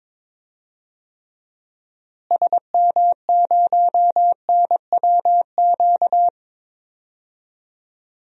Morse code, heard as a keyed tone, about 21 wpm, 700 Hz. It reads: SM0NWQ
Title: DXCC-100 Call Signs - 6 Characters - Repeated 3x - 22wpm